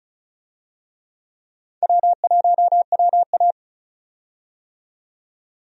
Morse code, heard as a keyed tone, about 36 wpm, 700 Hz. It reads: W1WA
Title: USA Call Signs - 4 Characters - 35wpm